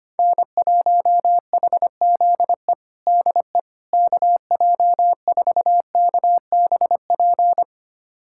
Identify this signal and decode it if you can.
Morse code, heard as a keyed tone, about 25 wpm, 700 Hz. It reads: N1HZE DE KJ4KBP
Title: USA Callsign DE Callsign 25wpm